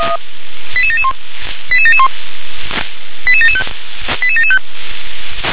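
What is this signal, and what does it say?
Five-tone selective call (ZVEI), heard as quick runs of five tones, tones of about 70 ms. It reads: 88771 88771 88775 88775
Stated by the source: ZVEI five-tone selective call received on SDR.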